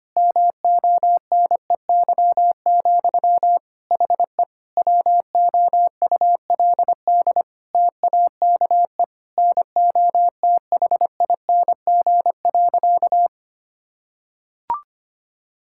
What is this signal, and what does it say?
Morse code, heard as a keyed tone, about 25 wpm, 700 Hz. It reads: MONEY, HE WOULD TAKE NOTHING.
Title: Sentences from Top 1000 Words - Review Part 6 - 25wpm